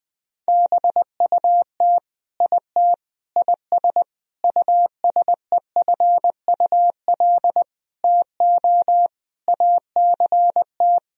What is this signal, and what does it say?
Morse code, heard as a keyed tone, about 20 wpm, 700 Hz. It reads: BUT IT IS USEFUL TO ACT
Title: Sentences from Top 1000 Words - Review Part 4 - 20wpm